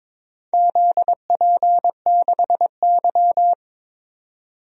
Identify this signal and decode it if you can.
Morse code, heard as a keyed tone, about 23 wpm, 700 Hz. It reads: ZP6Y
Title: DXCC-100 Call Signs - 4 Characters - 22wpm